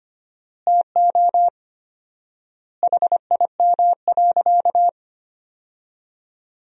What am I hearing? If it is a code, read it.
Morse code, heard as a keyed tone, about 25 wpm, 700 Hz. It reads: TO HIM.
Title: 4x Word-Spacing - Sentences from Top 400 Words - 25wpm